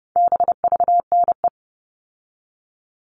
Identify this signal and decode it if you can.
Morse code, heard as a keyed tone, about 30 wpm, 700 Hz. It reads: BVNE
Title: All Cut Numbers - 4-Digit 30wpm